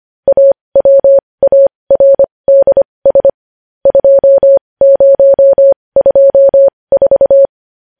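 Morse code, heard as a keyed tone, about 25 wpm, spaced as for 20 wpm, 550 Hz. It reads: AWARDS 2024
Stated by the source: Morse code news headlines practice at 20.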